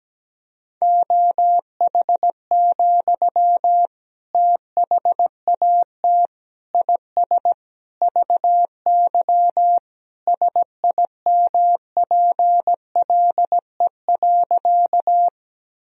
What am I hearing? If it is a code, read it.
Morse code, heard as a keyed tone, about 17 wpm, 700 Hz. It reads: OH, THAT IS VY SIMPLE.
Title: Sentences from Top 500 Words and Abbreviations 17wpm